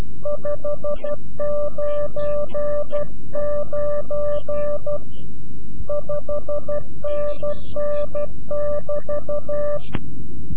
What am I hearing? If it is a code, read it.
Morse code, heard as a keyed tone, about 12 wpm, 590 Hz. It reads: 599 5NN <BT>
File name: CW-QSO / CQ Call / CQ-QSO